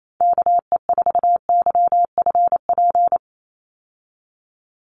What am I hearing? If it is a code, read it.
Morse code, heard as a keyed tone, about 28 wpm, 700 Hz. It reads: KE4YFP
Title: USA Call Signs 28wpm